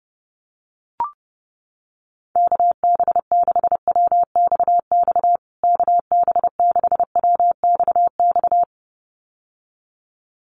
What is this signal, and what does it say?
Morse code, heard as a keyed tone, about 30 wpm, 700 Hz. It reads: KB6WXX KB6WXX
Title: USA Call Signs Repeated 2x 30wpm